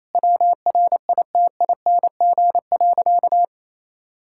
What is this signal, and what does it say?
Morse code, heard as a keyed tone, about 28 wpm, 700 Hz. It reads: WRITING.
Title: Sentences from Top 1600 Words 28wpm